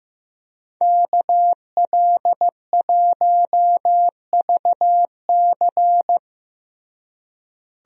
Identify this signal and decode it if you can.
Morse code, heard as a keyed tone, about 15 wpm, 700 Hz. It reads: KL1VC
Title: USA Call Signs - 5 Characters - Repeated 2x - 15wpm